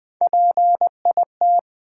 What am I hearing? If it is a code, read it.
Morse code, heard as a keyed tone, about 20 wpm, 700 Hz. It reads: PIT
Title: Build A Word - 1 Letter At A Time - 3-Letter Words - 20wpm